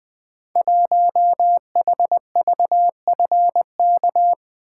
Morse code, heard as a keyed tone, about 20 wpm, 700 Hz. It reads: 1HVFK
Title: Code Groups - 5 Letter:Numbers - 20wpm